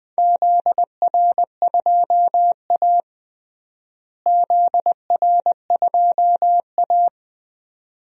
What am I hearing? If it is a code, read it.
Morse code, heard as a keyed tone, about 21 wpm, 700 Hz. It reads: ZR2A ZR2A
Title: DXCC-100 Call Signs - 4 Characters - Repeated 2x - 20wpm